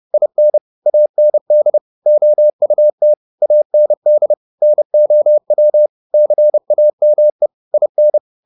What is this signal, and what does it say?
Morse code, heard as a keyed tone, about 30 wpm, 600 Hz. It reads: IN AND OUT AND NOW CAME IN